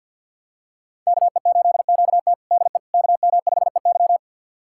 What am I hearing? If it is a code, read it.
Morse code, heard as a keyed tone, about 53 wpm, 700 Hz. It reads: KE8QT DE KM5EY
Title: USA Callsign DE Callsign - 5 Characters - 50wpm